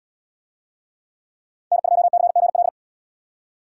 Morse code, heard as a keyed tone, about 77 wpm, 700 Hz. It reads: N2CGL